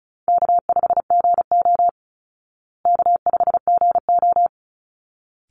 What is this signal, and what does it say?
Morse code, heard as a keyed tone, about 34 wpm, 700 Hz. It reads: K5GO K5GO
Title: Call Signs Repeated 3x - 35wpm